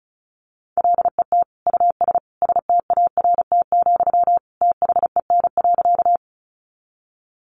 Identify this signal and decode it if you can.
Morse code, heard as a keyed tone, about 34 wpm, 700 Hz. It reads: LET US START, THEN.
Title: Sentences from Top 300 Words 35wpm